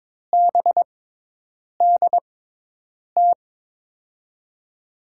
Morse code, heard as a keyed tone, about 22 wpm, spaced as for 8 wpm, 700 Hz. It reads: BDT
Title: USA Call Signs - 5 Characters - 6x Character Spacing - 22wpm